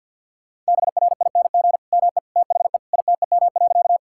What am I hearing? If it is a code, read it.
Morse code, heard as a keyed tone, about 50 wpm, 700 Hz. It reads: BRING ME THE ITEM.
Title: Sentences from Top 1000 Words - Review Part 6 - 50wpm